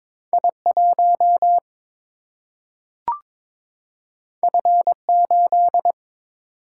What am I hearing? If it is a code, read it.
Morse code, heard as a keyed tone, about 22 wpm, 700 Hz. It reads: I 1 F 8
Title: Single-Letter Single-Number 22wpm